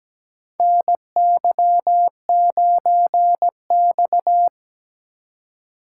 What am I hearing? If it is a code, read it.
Morse code, heard as a keyed tone, about 17 wpm, 700 Hz. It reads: NY9X
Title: K1USN SST QSO Practice - 17wpm